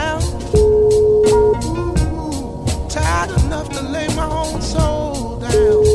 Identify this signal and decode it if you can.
Ringback tone (Europe, 425 Hz), with noise.